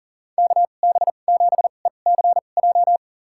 Morse code, heard as a keyed tone, about 40 wpm, spaced as for 29 wpm, 700 Hz. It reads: KB7ECJ
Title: USA Call Signs - 6 Characters - 2x Character Spacing - 40wpm